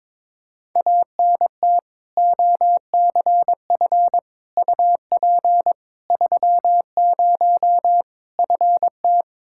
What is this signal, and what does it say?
Morse code, heard as a keyed tone, about 22 wpm, 700 Hz. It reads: ANT OCF UP 30 FT
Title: Practice QSO - Exchange 2 - Info - 22wpm